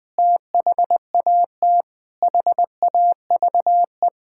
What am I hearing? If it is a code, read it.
Morse code, heard as a keyed tone, about 21 wpm, 700 Hz. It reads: THAT HAVE
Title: Sentences with 8-Word Phrases - 20wpm